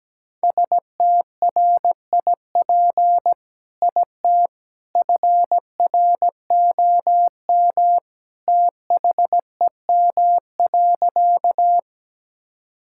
Morse code, heard as a keyed tone, about 17 wpm, 700 Hz. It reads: STRIP IT FROM THEM.
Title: Sentences from Top 1600 Words 17wpm